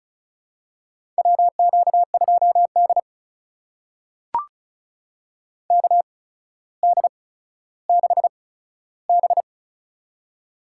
Morse code, heard as a keyed tone, about 36 wpm, spaced as for 15 wpm, 700 Hz. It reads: WQ2D KD6B
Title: USA Call Signs - 4 Characters - 8x Character Spacing - 35wpm